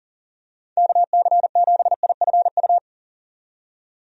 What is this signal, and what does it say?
Morse code, heard as a keyed tone, about 40 wpm, 700 Hz. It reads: KC7IFU
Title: USA Call Signs - 6 Characters - 40wpm